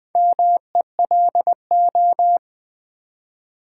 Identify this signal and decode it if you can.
Morse code, heard as a keyed tone, about 20 wpm, 700 Hz. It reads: MELO
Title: Build A Word - 2 Letters At A Time - 6-Letter Words - 20wpm